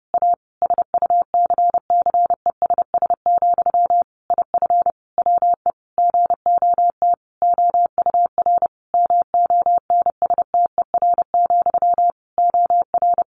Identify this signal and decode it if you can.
Morse code, heard as a keyed tone, about 30 wpm, 700 Hz. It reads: A SUCCESS, IF WE GOT OUR MONSTER, OR
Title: A Sound Of Thunder 30wpm